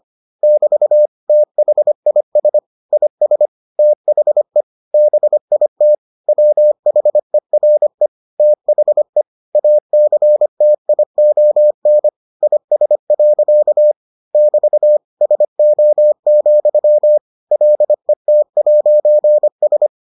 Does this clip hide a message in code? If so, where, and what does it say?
Yes: Morse code from 0.4 to 19.9 s, heard as a keyed tone, about 25 wpm, 600 Hz. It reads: <BT> THIS IS THE BIT WHERE THE ACTION IS. <BT> SO, LET'S